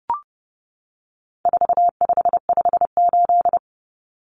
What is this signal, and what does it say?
Morse code, heard as a keyed tone, about 30 wpm, 700 Hz. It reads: 4558
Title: Numbers - 4-Digit 30wpm